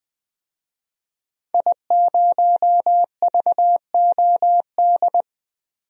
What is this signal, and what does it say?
Morse code, heard as a keyed tone, about 20 wpm, 700 Hz. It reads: I0VOD
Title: DXCC-100 Call Signs - 5 Characters - 20wpm